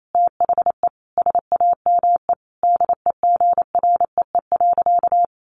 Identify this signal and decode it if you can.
Morse code, heard as a keyed tone, about 28 wpm, 700 Hz. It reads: THE SAME DEGREE.
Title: Sentences from Top 1500 Words 28wpm